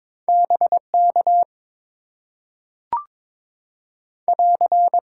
Morse code, heard as a keyed tone, about 22 wpm, 700 Hz. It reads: BK <AR>
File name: Top 90 QSO Elements - 6x Character Spacing - 22wpm